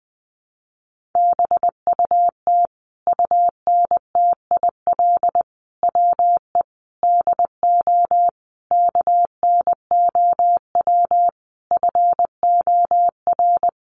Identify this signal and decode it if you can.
Morse code, heard as a keyed tone, about 20 wpm, 700 Hz. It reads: BUT UNTIL WE DO KNOW FOR
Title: A Sound Of Thunder 20wpm